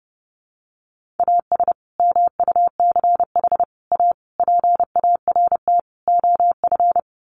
Morse code, heard as a keyed tone, about 30 wpm, 700 Hz. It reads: AS MUCH A PART OF
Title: Sets of 5 Words Encore2 30wpm